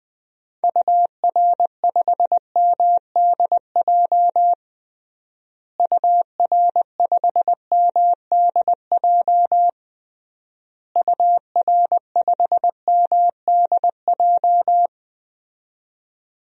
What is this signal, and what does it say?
Morse code, heard as a keyed tone, about 20 wpm, 700 Hz. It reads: UR5MDJ UR5MDJ UR5MDJ
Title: DXCC-100 Call Signs - 6 Characters - Repeated 3x - 20wpm